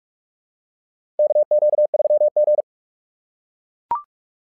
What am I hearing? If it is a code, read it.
Morse code, heard as a keyed tone, about 44 wpm, 600 Hz. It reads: KQ3G
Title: SOTA QSO Practice 45wpm